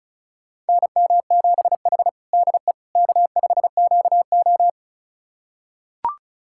Morse code, heard as a keyed tone, about 36 wpm, 700 Hz. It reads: NM7H DE K5QO
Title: USA Callsign DE Callsign - 4 Characters - 35wpm